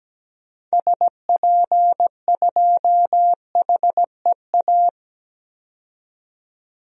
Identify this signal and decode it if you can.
Morse code, heard as a keyed tone, about 17 wpm, 700 Hz. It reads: SP2HEA
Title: DXCC-100 Call Signs - 6 Characters - 17wpm